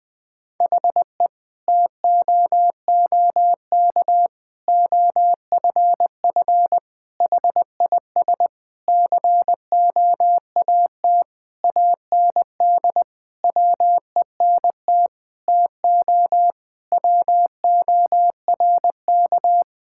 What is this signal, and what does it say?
Morse code, heard as a keyed tone, about 20 wpm, 700 Hz. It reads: HE TOOK OFF HIS COAT AND WENT TO WORK
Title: Sentences from Top 1600 Words 20wpm